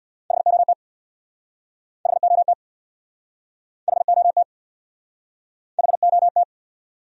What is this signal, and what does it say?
Morse code, heard as a keyed tone, about 69 wpm, 700 Hz. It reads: HOT HOT HOT HOT